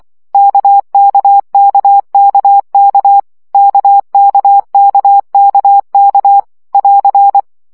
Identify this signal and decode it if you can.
Morse code, heard as a keyed tone, about 24 wpm, 800 Hz. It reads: KKKKK KKKKK <AR>
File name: CW gyakorlas k 20 baud